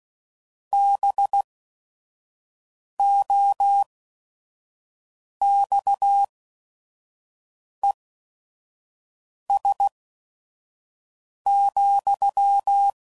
Morse code, heard as a keyed tone, about 16 wpm, spaced as for 5 wpm, 800 Hz. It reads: BOXES,